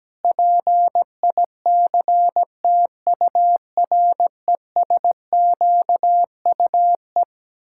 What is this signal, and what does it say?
Morse code, heard as a keyed tone, about 17 wpm, 700 Hz. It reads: PICTURESQUE
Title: Pangram Challenge v2 - 17wpm